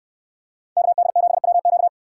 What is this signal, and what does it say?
Morse code, heard as a keyed tone, about 64 wpm, 700 Hz. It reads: KD7WQ